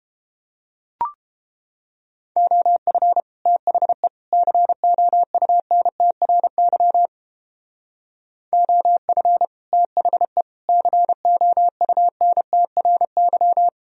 Morse code, heard as a keyed tone, about 29 wpm, 700 Hz. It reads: OF THE COUNTRY OF THE COUNTRY